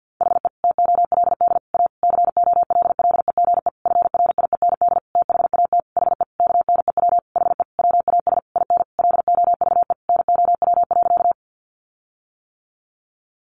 Morse code, heard as a keyed tone, about 50 wpm, 700 Hz. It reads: HE TOLD A COLLEGE FRIEND THAT HE KNEW HE WAS IN LOVE NOW.
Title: Sentences from Top 1300 Words 50wpm